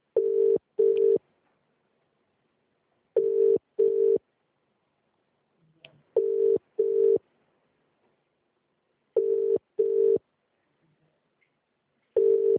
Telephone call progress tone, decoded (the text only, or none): ringback tone